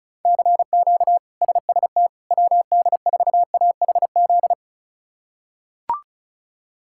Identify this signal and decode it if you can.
Morse code, heard as a keyed tone, about 34 wpm, 700 Hz. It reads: CQ SST WD4AHZ
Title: K1USN SST QSO Practice - 35wpm